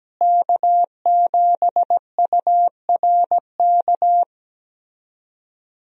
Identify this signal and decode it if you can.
Morse code, heard as a keyed tone, about 17 wpm, 700 Hz. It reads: K7URK
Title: USA Call Signs - 5 Characters - Repeated 2x - 17wpm